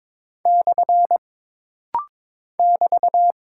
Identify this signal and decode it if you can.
Morse code, heard as a keyed tone, about 22 wpm, 700 Hz. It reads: / <BT>
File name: practice / 22wpm / Single Punctuation-Prosign - Rapid-Fire - 22wpm